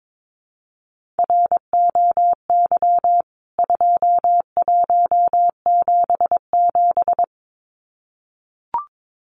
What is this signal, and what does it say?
Morse code, heard as a keyed tone, about 22 wpm, 700 Hz. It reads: ROY 2177
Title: CWT Exchange 22wpm